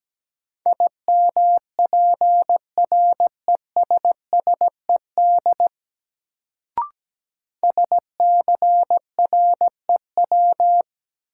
Morse code, heard as a keyed tone, about 17 wpm, 700 Hz. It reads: IMPRESSED SCREW
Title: Sentences from Top 1600 Words - New Word Drill - 17wpm